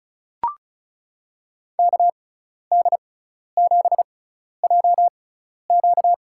Morse code, heard as a keyed tone, about 36 wpm, spaced as for 12 wpm, 700 Hz. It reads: KD7JQ
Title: USA Call Signs - 5 Characters - 6x Character Spacing - 35wpm